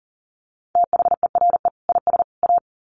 Morse code, heard as a keyed tone, about 40 wpm, 700 Hz. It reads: THERE IS A
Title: Sets of 3 Words 40wpm